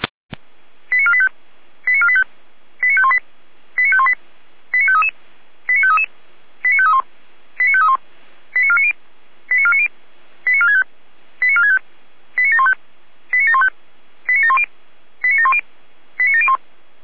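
Five-tone selective call (ZVEI), heard as quick runs of five tones, tones of about 70 ms. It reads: 89485 89485 89518 89518 89533 89533 89531 89531 89490 89490 89465 89465 89715 89715 89710 89710 89701
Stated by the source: ZVEI five-tone call 89416 received by SDR, feuerwehr-orlamuende.org fire brigade audio archive.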